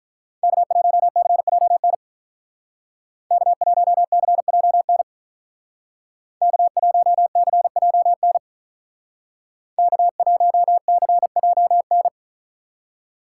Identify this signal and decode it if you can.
Morse code, heard as a keyed tone, about 47 wpm, 700 Hz. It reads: K1CJN K1CJN K1CJN K1CJN